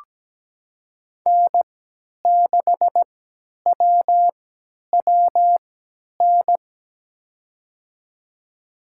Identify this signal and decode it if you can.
Morse code, heard as a keyed tone, about 17 wpm, spaced as for 10 wpm, 700 Hz. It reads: N6WWN
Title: USA Call Signs - 5 Characters - 3x Character Spacing - 17wpm